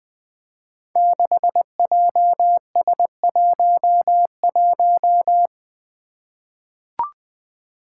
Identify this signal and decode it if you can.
Morse code, heard as a keyed tone, about 20 wpm, 700 Hz. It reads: 6JS11
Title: Five Challenging Letter-Number Combinations 20wpm